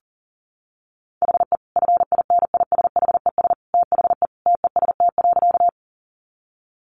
Morse code, heard as a keyed tone, about 40 wpm, 700 Hz. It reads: HE FINISHES THE TEST.
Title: Additional Easy Phrases 40wpm